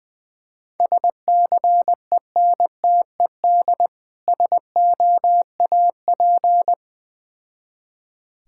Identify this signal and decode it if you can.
Morse code, heard as a keyed tone, about 21 wpm, 690 Hz. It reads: SCENTED SOAP
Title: Difficult Words with Hints - 20wpm